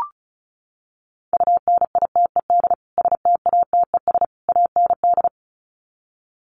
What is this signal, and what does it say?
Morse code, heard as a keyed tone, about 34 wpm, 700 Hz. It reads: UNITED STATES AND
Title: Sets of 3 Words 35wpm